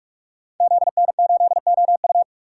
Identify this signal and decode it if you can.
Morse code, heard as a keyed tone, about 44 wpm, 690 Hz. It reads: ZN8OU
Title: Code Groups - 5 Letter:Numbers - 45wpm